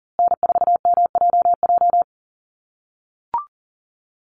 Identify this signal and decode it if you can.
Morse code, heard as a keyed tone, about 40 wpm, 700 Hz. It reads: N4MJJ
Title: USA Call Signs Repeated 3x 40wpm